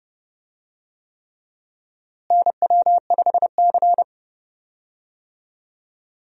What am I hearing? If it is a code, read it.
Morse code, heard as a keyed tone, about 30 wpm, 700 Hz. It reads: NW5C
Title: USA Call Signs - 4 Characters - Repeated 3x - 30wpm